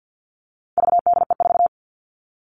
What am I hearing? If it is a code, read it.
Morse code, heard as a keyed tone, about 50 wpm, 700 Hz. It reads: VDE4
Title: All Cut Numbers - 4-Digit 50wpm